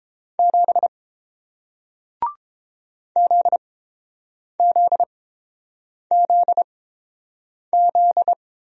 Morse code, heard as a keyed tone, about 30 wpm, spaced as for 7 wpm, 700 Hz. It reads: 7 ZZZZ